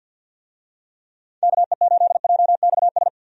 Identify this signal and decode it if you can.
Morse code, heard as a keyed tone, about 50 wpm, 700 Hz. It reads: KE8JXS
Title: USA Call Signs - 6 Characters - 50wpm